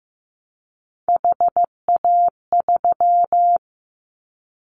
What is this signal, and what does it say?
Morse code, heard as a keyed tone, about 15 wpm, 700 Hz. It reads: HA3
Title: Callsign Prefixes for Top 50 DXCC - 15wpm